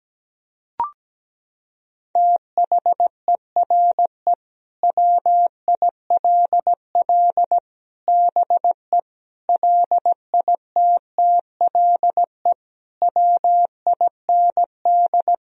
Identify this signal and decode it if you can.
Morse code, heard as a keyed tone, about 17 wpm, 700 Hz. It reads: THERE WILL BE LITTLE WIND
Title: Sentences from Top 1000 Words - Review Part 2 - 17wpm